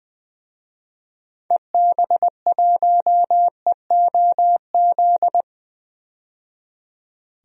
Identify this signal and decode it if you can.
Morse code, heard as a keyed tone, about 21 wpm, 700 Hz. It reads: EB1EOZ
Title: DXCC-100 Call Signs - 6 Characters - 20wpm